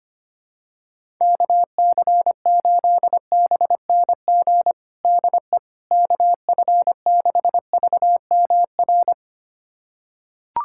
Morse code, heard as a keyed tone, about 25 wpm, 700 Hz. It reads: KC8BNG DE KF6VMR
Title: USA Callsign DE Callsign - 6 Characters - 25wpm